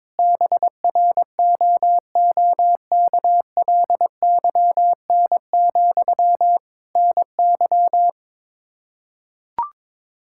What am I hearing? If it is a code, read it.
Morse code, heard as a keyed tone, about 22 wpm, 700 Hz. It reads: BROOKLYN, NY
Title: Practice QSO - Exchange 1 - Beginning - 22wpm